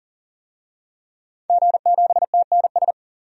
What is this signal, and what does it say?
Morse code, heard as a keyed tone, about 38 wpm, 700 Hz. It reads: G7TNS
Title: DXCC-100 Call Signs - 5 Characters - 40wpm